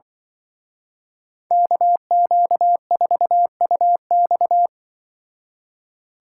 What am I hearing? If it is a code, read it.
Morse code, heard as a keyed tone, about 24 wpm, 700 Hz. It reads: KQ4UX
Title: Speed Racing - USA 2x2 Call Signs 20wpm